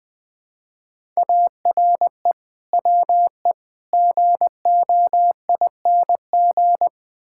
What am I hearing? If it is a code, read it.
Morse code, heard as a keyed tone, about 20 wpm, 700 Hz. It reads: ARE WE GOING
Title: Build A Phrase - 4-Word Phrases - 20wpm